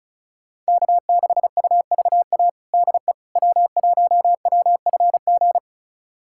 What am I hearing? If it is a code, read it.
Morse code, heard as a keyed tone, about 36 wpm, 690 Hz. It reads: K6UVA DE W1WFG
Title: USA Callsign DE Callsign - 5 Characters - 35wpm